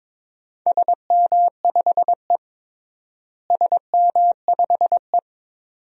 Morse code, heard as a keyed tone, about 22 wpm, 700 Hz. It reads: SM5E SM5E
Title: DXCC-100 Call Signs - 4 Characters - Repeated 2x - 22wpm